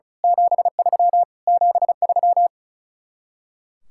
Morse code, heard as a keyed tone, about 33 wpm, 700 Hz. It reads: 73 73
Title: Speed-Racing Common QSO Phrases Encore - 35wpm